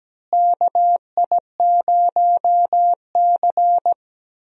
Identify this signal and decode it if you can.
Morse code, heard as a keyed tone, about 17 wpm, 700 Hz. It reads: KI0C
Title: USA Call Signs - 4 Characters - Repeated 2x - 17wpm